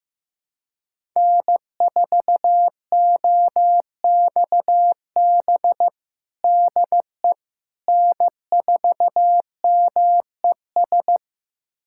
Morse code, heard as a keyed tone, about 16 wpm, 700 Hz. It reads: N4OXB DE N4MES
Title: USA Callsign DE Callsign - 5 Characters - 15wpm